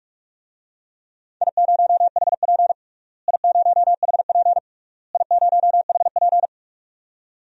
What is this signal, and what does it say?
Morse code, heard as a keyed tone, about 43 wpm, 700 Hz. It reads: I0HP I0HP I0HP
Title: DXCC-100 Call Signs - 4 Characters - Repeated 3x - 45wpm